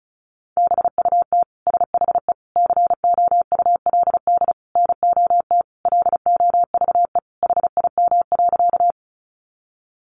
Morse code, heard as a keyed tone, about 34 wpm, 700 Hz. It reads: BUT SHE COULD NOT LOVE HIM.
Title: Sentences from Top 600 Words 35wpm